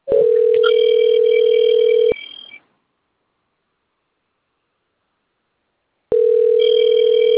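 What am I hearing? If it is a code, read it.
Ringback tone (North America), 440 + 480 Hz.
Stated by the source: Telephone call recording from the Bebedouro 1 system.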